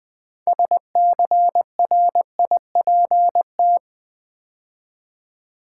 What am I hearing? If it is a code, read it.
Morse code, heard as a keyed tone, about 20 wpm, 700 Hz. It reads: SCRIPT
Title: Build A Word - 2 Letters At A Time - 6-Letter Words - 20wpm